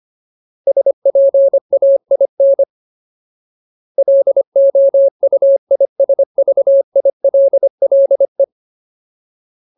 Morse code, heard as a keyed tone, about 25 wpm, 550 Hz. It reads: SPAIN LOUISVILLE
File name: Ultimate Word List / 25WPM Ultimate